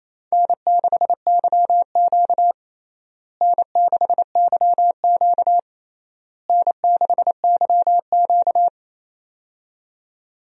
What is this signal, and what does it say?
Morse code, heard as a keyed tone, about 28 wpm, 700 Hz. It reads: N6YQ N6YQ N6YQ
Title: USA Call Signs - 4 Characters - Repeated 3x - 28wpm